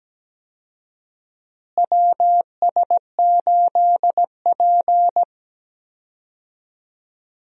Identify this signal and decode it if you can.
Morse code, heard as a keyed tone, about 17 wpm, 700 Hz. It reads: WS8P
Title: USA Call Signs - 4 Characters - Repeated 2x - 17wpm